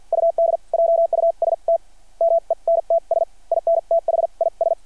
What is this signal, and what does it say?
Morse code, heard as a keyed tone, about 52 wpm, spaced as for 38 wpm, 660 Hz. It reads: UD JUSTMENTS IN THIS
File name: archived / CRSnet